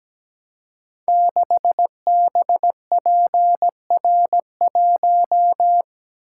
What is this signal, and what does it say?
Morse code, heard as a keyed tone, about 17 wpm, 700 Hz. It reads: 6BPR1
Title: Code Groups - 5 Letter:Numbers - 17wpm